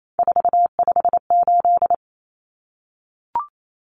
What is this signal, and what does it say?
Morse code, heard as a keyed tone, about 28 wpm, 700 Hz. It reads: 458
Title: Numbers - 3-Digit 28wpm